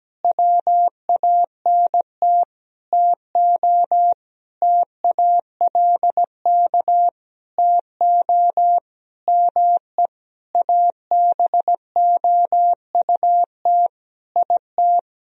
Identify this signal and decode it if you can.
Morse code, heard as a keyed tone, about 17 wpm, 700 Hz. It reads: WANT TO TALK TO ME ABOUT IT
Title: Sentences from Top 1000 Words - Review Part 4 - 17wpm